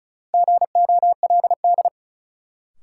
Morse code, heard as a keyed tone, about 36 wpm, 700 Hz. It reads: GOLD
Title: Sentences from Top 1000 Words - Review Part 3 - New Word Drill - 35wpm